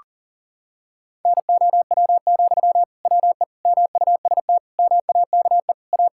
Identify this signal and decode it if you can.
Morse code, heard as a keyed tone, about 37 wpm, 700 Hz. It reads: NOW, WE MUST MAKE A
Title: Sentences from Top 1000 Words - Review Part 6 - 40wpm